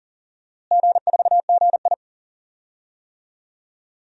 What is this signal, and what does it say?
Morse code, heard as a keyed tone, about 40 wpm, 700 Hz. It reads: G4GI
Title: DXCC-100 Call Signs - 4 Characters - Repeated 3x - 40wpm